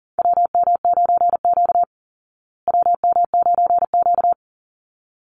Morse code, heard as a keyed tone, about 40 wpm, 700 Hz. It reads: WM9Q WM9Q
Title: Call Signs Repeated 3x - 40wpm